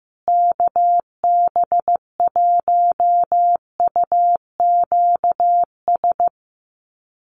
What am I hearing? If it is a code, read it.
Morse code, heard as a keyed tone, about 15 wpm, 700 Hz. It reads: KB1UQS
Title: USA Call Signs Repeated 3x 15wpm